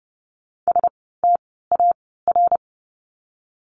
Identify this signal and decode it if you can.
Morse code, heard as a keyed tone, about 30 wpm, spaced as for 17 wpm, 700 Hz. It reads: STAR
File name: Lesson 17f - Letter u - Words 30wpm